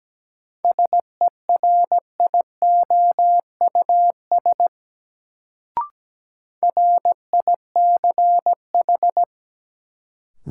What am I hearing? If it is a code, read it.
Morse code, heard as a keyed tone, about 17 wpm, 700 Hz. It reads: SERIOUS RICH
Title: Sentences from Top 1100 Words - New Word Drill - 17wpm